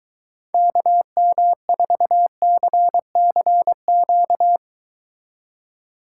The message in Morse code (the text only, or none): KM4CCQ